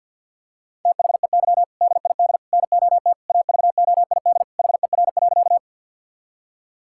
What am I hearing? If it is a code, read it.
Morse code, heard as a keyed tone, about 50 wpm, 700 Hz. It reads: THEY DID NOT AVOID HER.
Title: Sentences from Top 1000 Words - Review Part 6 - 50wpm